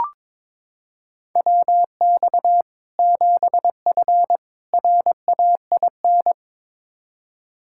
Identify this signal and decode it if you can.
Morse code, heard as a keyed tone, about 22 wpm, 700 Hz. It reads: WX 7F RAIN
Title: Practice QSO - Exchange 2 - Info - 22wpm